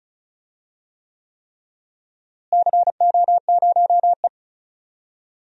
Morse code, heard as a keyed tone, about 36 wpm, 700 Hz. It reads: CO0E
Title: DXCC-100 Call Signs - 4 Characters - Repeated 3x - 35wpm